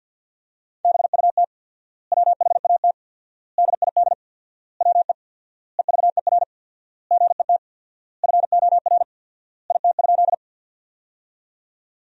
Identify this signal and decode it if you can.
Morse code, heard as a keyed tone, about 50 wpm, 700 Hz. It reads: BUT WHAT DID WE EVER GET FOR IT?
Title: 4x Word-Spacing - Sentences from Top 400 Words - 50wpm